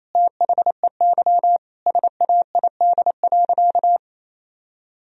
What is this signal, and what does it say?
Morse code, heard as a keyed tone, about 28 wpm, 700 Hz. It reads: THEY SAID.
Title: Sentences from Top 1000 Words - Review Part 2 - 28wpm